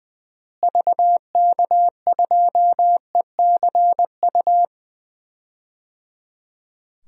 Morse code, heard as a keyed tone, about 21 wpm, 700 Hz. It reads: VK2ECU
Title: DXCC-100 Call Signs - 6 Characters - Repeated 3x - 20wpm